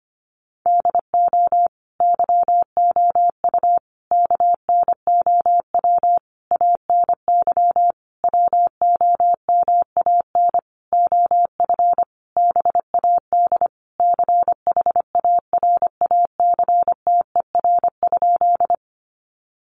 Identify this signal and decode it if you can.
Morse code, heard as a keyed tone, about 25 wpm, 700 Hz. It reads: DO YOU KNOW ANY WOMAN OF BAD CHARACTER?
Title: Sentences from Top 1200 Words 25wpm